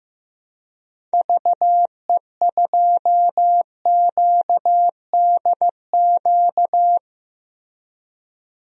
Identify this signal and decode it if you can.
Morse code, heard as a keyed tone, about 15 wpm, 700 Hz. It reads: VE2QDQ
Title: DXCC-100 Call Signs - 6 Characters - Repeated 2x - 15wpm